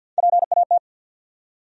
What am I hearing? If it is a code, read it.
Morse code, heard as a keyed tone, about 52 wpm, 700 Hz. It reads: PAT
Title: USA Common Names - Max 3 Letters - 50wpm